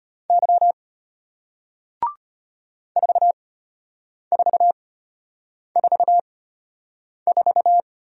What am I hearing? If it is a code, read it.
Morse code, heard as a keyed tone, about 34 wpm, spaced as for 8 wpm, 700 Hz. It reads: Y 4444